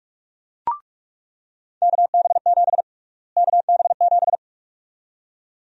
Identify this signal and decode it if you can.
Morse code, heard as a keyed tone, about 46 wpm, 700 Hz. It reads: KB7 KB7
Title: USA Call Sign Prefixes - Repeated 2x - 45wpm